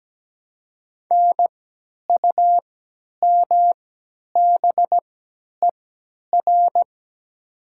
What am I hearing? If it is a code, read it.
Morse code, heard as a keyed tone, about 17 wpm, spaced as for 10 wpm, 700 Hz. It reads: NUMBER
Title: 3x Character Spacing - Top 300 Words - 17wpm